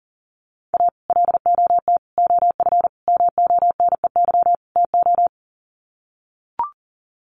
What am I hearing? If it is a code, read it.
Morse code, heard as a keyed tone, about 40 wpm, 700 Hz. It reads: A LOT OF MONEY TO
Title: Sets of 5 Words Encore2 40wpm